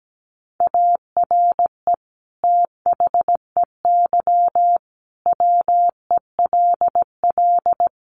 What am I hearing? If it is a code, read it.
Morse code, heard as a keyed tone, about 17 wpm, 700 Hz. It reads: ARE THEY WELL
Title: Sentences from Top 1300 Words 17wpm